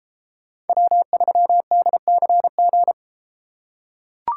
Morse code, heard as a keyed tone, about 32 wpm, 700 Hz. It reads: W3DCG